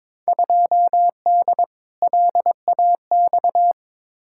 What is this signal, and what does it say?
Morse code, heard as a keyed tone, about 22 wpm, 700 Hz. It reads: 2D LAX
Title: Field Day Exchanges - 22wpm